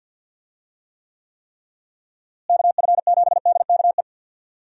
Morse code, heard as a keyed tone, about 51 wpm, 700 Hz. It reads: KF7DKE
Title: USA Call Signs - 6 Characters - Repeated 3x - 50wpm